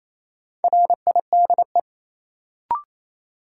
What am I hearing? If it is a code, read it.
Morse code, heard as a keyed tone, about 28 wpm, 700 Hz. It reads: RIDE
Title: Sentences from Top 1000 Words - Review Part 6 - New Word Drill - 28wpm